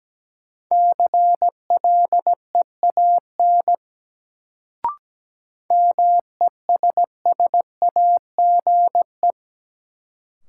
Morse code, heard as a keyed tone, about 17 wpm, 700 Hz. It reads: CLEAN MESSAGE
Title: Sentences from Top 1100 Words - New Word Drill - 17wpm